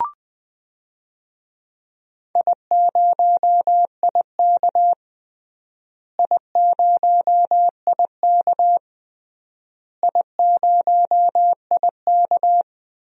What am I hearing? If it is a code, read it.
Morse code, heard as a keyed tone, about 20 wpm, 700 Hz. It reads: I0IK I0IK I0IK
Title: DXCC-100 Call Signs - 4 Characters - Repeated 3x - 20wpm